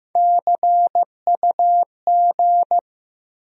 Morse code, heard as a keyed tone, about 15 wpm, 700 Hz. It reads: CUG
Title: USA Call Sign Postfixes - 3 Letters 15wpm